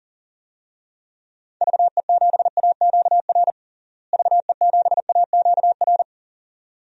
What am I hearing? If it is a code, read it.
Morse code, heard as a keyed tone, about 40 wpm, 700 Hz. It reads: VE7AQR VE7AQR
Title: DXCC-100 Call Signs - 6 Characters - Repeated 2x - 40wpm